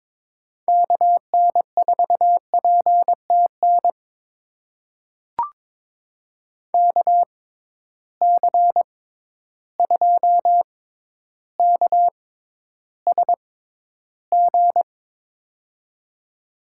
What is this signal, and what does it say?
Morse code, heard as a keyed tone, about 22 wpm, spaced as for 11 wpm, 700 Hz. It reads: KN4PTN KC2KSG
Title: USA Call Signs - 6 Characters - 6x Character Spacing - 22wpm